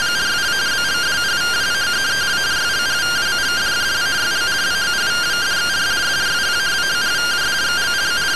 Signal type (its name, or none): two-tone teleprinter signal